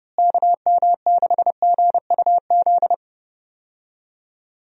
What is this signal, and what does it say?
Morse code, heard as a keyed tone, about 30 wpm, 700 Hz. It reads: KM6GUZ
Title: USA Call Signs - 6 Characters - Repeated 2x - 30wpm